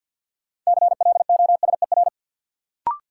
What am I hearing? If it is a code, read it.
Morse code, heard as a keyed tone, about 47 wpm, 700 Hz. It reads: CLOSER
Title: Sets of 5 Words Encore - New Word Drill - 50wpm